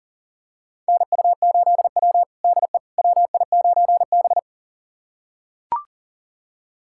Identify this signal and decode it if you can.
Morse code, heard as a keyed tone, about 40 wpm, 700 Hz. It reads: NU8W DE WI9B
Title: USA Callsign DE Callsign - 4 Characters - 40wpm